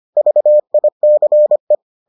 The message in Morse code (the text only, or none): VICE